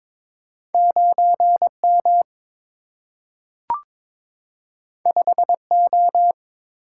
Morse code, heard as a keyed tone, about 22 wpm, 700 Hz. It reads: 9 M 5 O
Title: Single-Letter Single-Number 22wpm